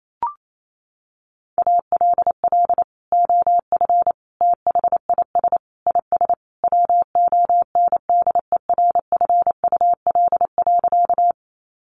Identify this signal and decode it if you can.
Morse code, heard as a keyed tone, about 28 wpm, 700 Hz. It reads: ALL OF THIS IS WONDERFUL.
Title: Sentences from Top 1100 Words 28wpm